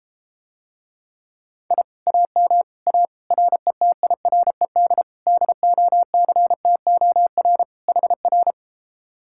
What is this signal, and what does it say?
Morse code, heard as a keyed tone, about 32 wpm, 700 Hz. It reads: I AM A RETIRED DOCTOR HR